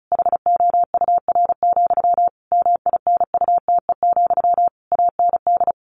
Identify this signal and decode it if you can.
Morse code, heard as a keyed tone, about 34 wpm, 700 Hz. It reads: HOUR, MINUTE, AND
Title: A Sound Of Thunder 35wpm